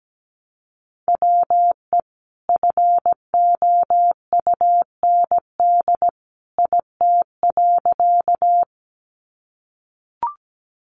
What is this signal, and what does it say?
Morse code, heard as a keyed tone, about 17 wpm, 700 Hz. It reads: WE FOUND IT.
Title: Sentences from Top 300 Words 17wpm